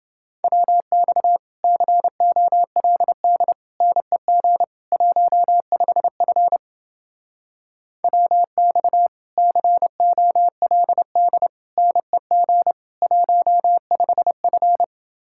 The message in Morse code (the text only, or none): WX COLD NEG 15F WX COLD NEG 15F